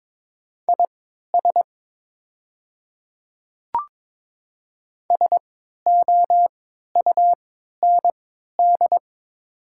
Morse code, heard as a keyed tone, about 22 wpm, spaced as for 13 wpm, 700 Hz. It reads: IS SOUND
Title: 3x Character Spacing - Top 100 Words - 22wpm